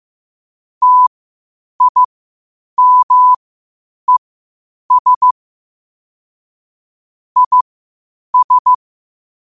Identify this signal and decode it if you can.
Morse code, heard as a keyed tone, about 15 wpm, spaced as for 8 wpm, 1000 Hz. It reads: TIMES IS
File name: 8wpm